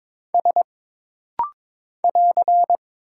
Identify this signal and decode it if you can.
Morse code, heard as a keyed tone, about 22 wpm, 700 Hz. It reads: S <AR>
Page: This rapid-fire timing is geared to push your ICR (Instant Character Recognition) skills.